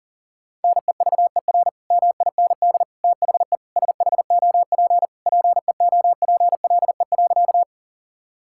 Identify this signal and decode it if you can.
Morse code, heard as a keyed tone, about 40 wpm, 700 Hz. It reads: NEVER MIND THE SHOP PEOPLE.
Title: Sentences from Top 1600 Words 40wpm